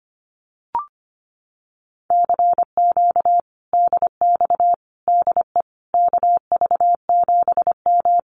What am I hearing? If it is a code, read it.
Morse code, heard as a keyed tone, about 25 wpm, 700 Hz. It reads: CQ DX DE KV7M
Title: Common QSO Phrases Encore 25wpm